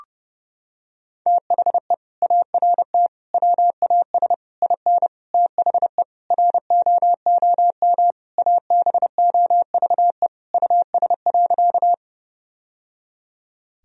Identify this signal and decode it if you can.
Morse code, heard as a keyed tone, about 30 wpm, 700 Hz. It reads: THE ART WAS IN THE ROOM ABOVE US.
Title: Sentences from Top 1000 Words - Review Part 3 - 30wpm